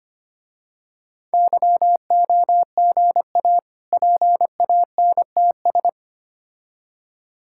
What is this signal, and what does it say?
Morse code, heard as a keyed tone, about 25 wpm, 700 Hz. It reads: YOGA PANTS
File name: practice / 25wpm / Difficult Words - Speed-Racing with Hints - 25wpm